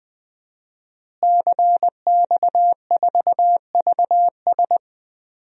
Morse code, heard as a keyed tone, about 20 wpm, 700 Hz. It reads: CX4VS
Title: DXCC-100 Call Signs - 5 Characters - Repeated 3x - 20wpm